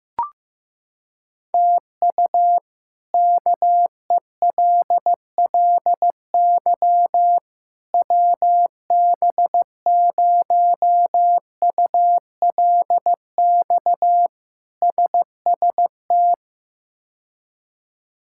Morse code, heard as a keyed tone, about 15 wpm, 700 Hz. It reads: TU KELLY WB0ULX SST
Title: K1USN SST QSO Practice - 15wpm